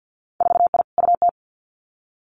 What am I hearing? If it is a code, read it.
Morse code, heard as a keyed tone, about 50 wpm, 700 Hz. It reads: 4I UT
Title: Winter Field Day Exchanges 50wpm